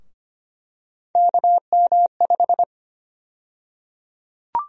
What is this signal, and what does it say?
Morse code, heard as a keyed tone, about 25 wpm, 700 Hz. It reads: KM5
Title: USA Call Sign Prefixes - Repeated 2x - 25wpm